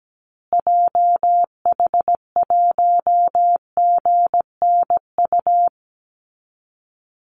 Morse code, heard as a keyed tone, about 17 wpm, 700 Hz. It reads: JH1GNU
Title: Call Signs Repeated 2x - 17wpm